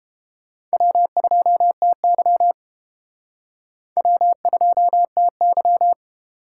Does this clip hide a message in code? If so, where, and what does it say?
Yes: Morse code from 0.7 to 5.9 s, heard as a keyed tone, about 29 wpm, 700 Hz. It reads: W2TY W2TY